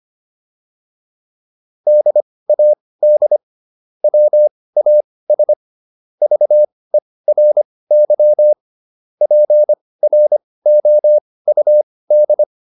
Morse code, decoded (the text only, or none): DAD WAS VERY PROUD